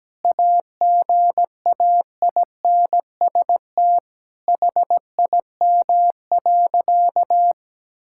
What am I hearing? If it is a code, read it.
Morse code, heard as a keyed tone, about 17 wpm, 700 Hz. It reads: AGAINST HIM.
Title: Sentences from Top 1600 Words 17wpm